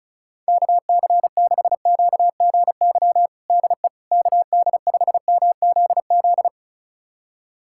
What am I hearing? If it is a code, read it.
Morse code, heard as a keyed tone, about 36 wpm, 700 Hz. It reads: KC6QGY DE KD5MZZ
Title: USA Callsign DE Callsign - 6 Characters - 35wpm